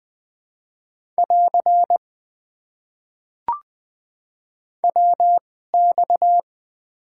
Morse code, heard as a keyed tone, about 20 wpm, spaced as for 14 wpm, 700 Hz. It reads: <AR> WX
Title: Top 90 QSO Elements - 2x Character Spacing - 20wpm